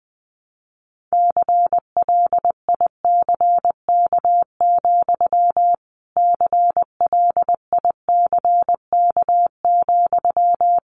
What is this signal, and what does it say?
Morse code, heard as a keyed tone, about 20 wpm, 700 Hz. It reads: CLICK, CLICK,
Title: Sentences from Top 1100 Words 20wpm